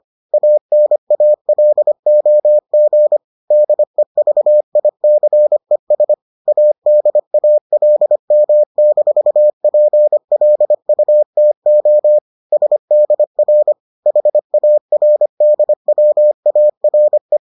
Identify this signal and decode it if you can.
Morse code, heard as a keyed tone, about 25 wpm, 600 Hz. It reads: ANALOG DEVICES ADALM-PLUTO SDR HARDWARE